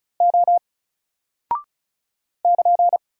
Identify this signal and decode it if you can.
Morse code, heard as a keyed tone, about 36 wpm, 700 Hz. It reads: O <KN>
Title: Single Character (Letter:Number:Punctuation:Prosign) Alt 35wpm